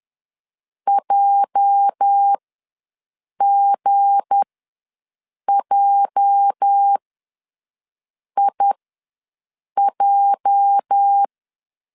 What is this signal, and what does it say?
Morse code, heard as a keyed tone, about 10 wpm, spaced as for 6 wpm, 800 Hz. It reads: JGJIJ
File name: LEZIONI / 8 YJ45 / 8 YJ ws 10wpm 4043